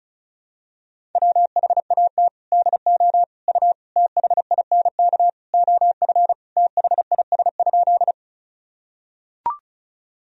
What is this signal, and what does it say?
Morse code, heard as a keyed tone, about 34 wpm, 700 Hz. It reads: WHAT DO U THINK OF THIS?
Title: Sentences from Top 200 Words and Abbreviations 35wpm